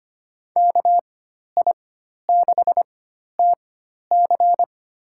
Morse code, heard as a keyed tone, about 25 wpm, spaced as for 12 wpm, 700 Hz. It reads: KI6TC
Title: USA Call Signs - 5 Characters - 4x Character Spacing - 25wpm